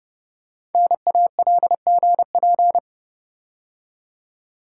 Morse code, heard as a keyed tone, about 30 wpm, 700 Hz. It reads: NALGP
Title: Code Groups - 5 Letters - 30wpm